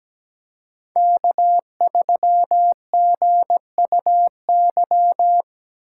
Morse code, heard as a keyed tone, about 17 wpm, 700 Hz. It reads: K3GUY
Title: USA Call Signs - 3x Character Spacing - 17wpm